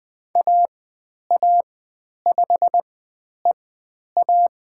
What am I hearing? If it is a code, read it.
Morse code, heard as a keyed tone, about 20 wpm, spaced as for 10 wpm, 700 Hz. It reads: AA5EA
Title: 102 Lesson 17g - Letter u - Call Signs 20wpm